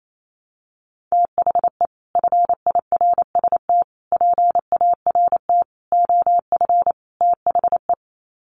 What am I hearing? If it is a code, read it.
Morse code, heard as a keyed tone, about 28 wpm, 700 Hz. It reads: THE FIRST PART OF THE
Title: Sets of 5 Words Encore 28wpm